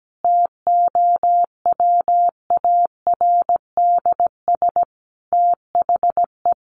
Morse code, heard as a keyed tone, about 17 wpm, 700 Hz. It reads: TOWARDS THE
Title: Sentences from Top 1100 Words 17wpm